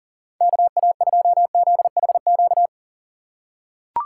Morse code, heard as a keyed tone, about 40 wpm, 700 Hz. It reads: KA2ZHQ
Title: USA Call Signs - 6 Characters - 2x Character Spacing - 40wpm